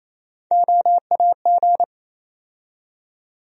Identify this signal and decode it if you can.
Morse code, heard as a keyed tone, about 28 wpm, 700 Hz. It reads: OAG
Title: USA Call Sign Postfixes - 3 Letters 28wpm